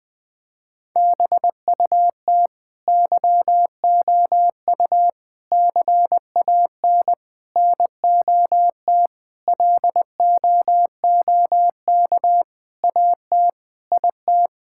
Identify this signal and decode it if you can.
Morse code, heard as a keyed tone, about 21 wpm, 700 Hz. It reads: BUT YOU CAN NOT LOOK AT IT
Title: Sentences from Top 1000 Words - Review Part 6 - 20wpm